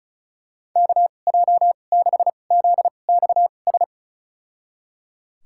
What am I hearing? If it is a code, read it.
Morse code, heard as a keyed tone, about 34 wpm, spaced as for 25 wpm, 700 Hz. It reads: KJ6ZXS
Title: USA Call Signs - 6 Characters - 2x Character Spacing - 35wpm